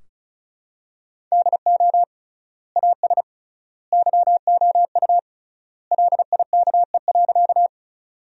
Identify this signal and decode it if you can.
Morse code, heard as a keyed tone, about 34 wpm, 700 Hz. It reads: DO AS YOU LIKE.
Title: 3x Word-Spacing - Sentences from Top 100 Words - 35wpm